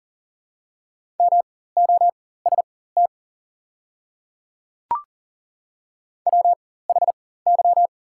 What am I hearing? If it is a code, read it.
Morse code, heard as a keyed tone, about 40 wpm, spaced as for 19 wpm, 700 Hz. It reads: MOST WHY
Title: 4x Character Spacing - Top 200 Words - 40wpm